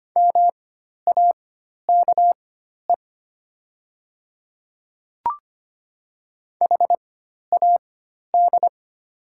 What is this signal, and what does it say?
Morse code, heard as a keyed tone, about 25 wpm, spaced as for 12 wpm, 700 Hz. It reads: MAKE HAD
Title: 4x Character Spacing - Top 500 Words - 25wpm